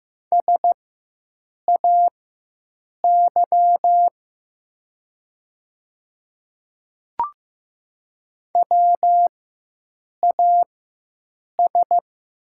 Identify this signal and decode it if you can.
Morse code, heard as a keyed tone, about 15 wpm, spaced as for 7 wpm, 700 Hz. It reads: SAY WAS
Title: 4x Character Spacing - Top 200 Words - 15wpm